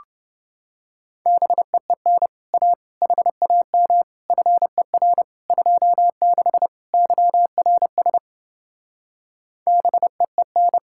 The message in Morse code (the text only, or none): BEEN A HAM FER 26 YRS BEEN